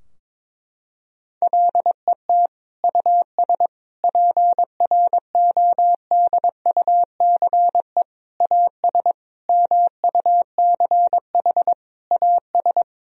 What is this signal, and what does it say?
Morse code, heard as a keyed tone, about 22 wpm, 700 Hz. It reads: LET US PRODUCE AS MUCH AS
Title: Sentences from Top 1000 Words - Review Part 3 - 22wpm